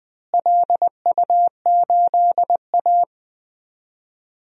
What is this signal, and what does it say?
Morse code, heard as a keyed tone, about 20 wpm, 700 Hz. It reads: LU8A
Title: DXCC-100 Call Signs - 4 Characters - Repeated 3x - 20wpm